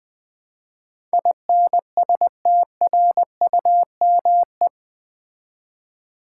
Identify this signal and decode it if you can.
Morse code, heard as a keyed tone, about 20 wpm, 700 Hz. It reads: INSTRUME
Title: Build A Word - 4 Letters At A Time - 10-Letter Words - 20wpm